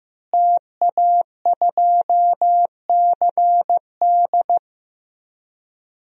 Morse code, heard as a keyed tone, about 15 wpm, 700 Hz. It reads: TA2CD
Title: DXCC-100 Call Signs - 5 Characters - Repeated 2x - 15wpm